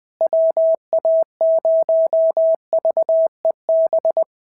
Morse code, heard as a keyed tone, about 18 wpm, 650 Hz. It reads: WA0VEB
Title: SOTA QSO Practice 20wpm